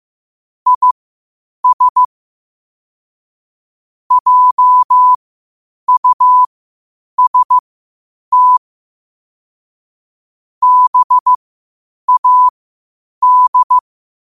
Morse code, heard as a keyed tone, about 15 wpm, spaced as for 8 wpm, 1000 Hz. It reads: IS JUST BAD
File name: qotdfiles / 8wpm